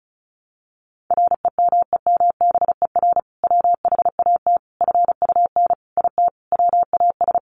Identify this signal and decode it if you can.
Morse code, heard as a keyed tone, about 34 wpm, 700 Hz. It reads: REMEMBER WHAT FUN IT WAS
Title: Sentences from Top 700 Words 35wpm